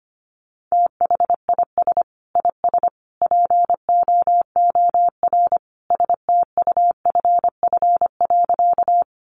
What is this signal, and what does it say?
Morse code, heard as a keyed tone, about 25 wpm, 700 Hz. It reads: THIS IS POOR STUFF.
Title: Sentences from Top 1400 Words 25wpm